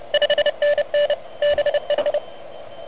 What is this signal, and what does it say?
Morse code, heard as a keyed tone, about 31 wpm, 590 Hz. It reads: 5NN BH
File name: truSDX